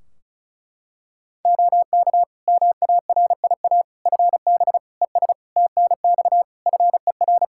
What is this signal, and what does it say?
Morse code, heard as a keyed tone, about 36 wpm, 700 Hz. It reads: OK MARIA FB ES TNX FER
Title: Practice QSO - Exchange 2 - Info - 35wpm